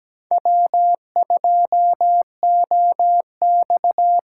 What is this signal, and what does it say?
Morse code, heard as a keyed tone, about 17 wpm, 700 Hz. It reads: W2OX
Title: USA Call Signs - 3x Character Spacing - 17wpm